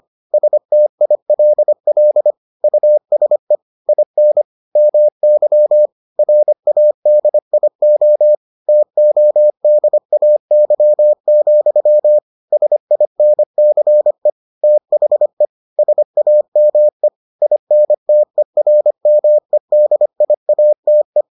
Morse code, heard as a keyed tone, about 25 wpm, 600 Hz. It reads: STILL USE IN MY RADIO TODAY, SINCE THE SAME INTERMEDIATE